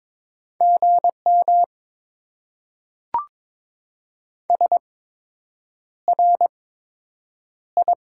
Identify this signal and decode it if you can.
Morse code, heard as a keyed tone, about 22 wpm, spaced as for 8 wpm, 690 Hz. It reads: GM SRI
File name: Top 90 QSO Elements - 8x Character Spacing - 22wpm